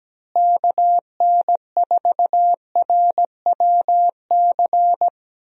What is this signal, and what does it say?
Morse code, heard as a keyed tone, about 17 wpm, 700 Hz. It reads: KN4RWC
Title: USA Call Signs - 6 Characters - Repeated 2x - 17wpm